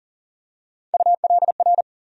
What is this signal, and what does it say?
Morse code, heard as a keyed tone, about 40 wpm, 700 Hz. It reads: ULR
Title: USA Call Sign Postfixes - 3 Letters 40wpm